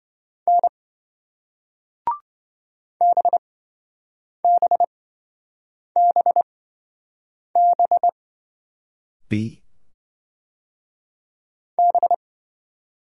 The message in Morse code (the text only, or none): N BBBB B